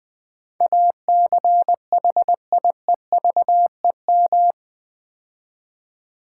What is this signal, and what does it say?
Morse code, heard as a keyed tone, about 20 wpm, 700 Hz. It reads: ACHIEVEM
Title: Build A Word - 4 Letters At A Time - 11-Letter Words - 20wpm